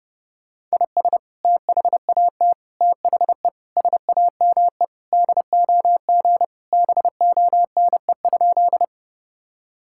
Morse code, heard as a keyed tone, about 31 wpm, 700 Hz. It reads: IS THAT THE SAME DOG BONE?
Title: Sentences from Top 1600 Words 30wpm